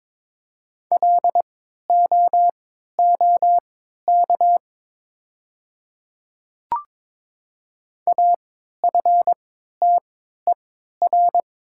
Morse code, heard as a keyed tone, about 21 wpm, spaced as for 12 wpm, 700 Hz. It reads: LOOK AFTER
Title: 3x Character Spacing - Top 500 Words - 22wpm